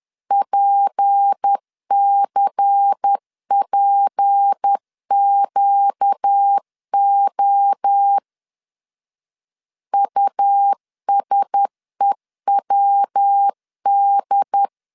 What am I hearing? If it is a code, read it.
Morse code, heard as a keyed tone, about 10 wpm, 800 Hz. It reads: PCPQO USEWD
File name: LEZIONI / 6 PKX0 / 6 P 10wpm 2432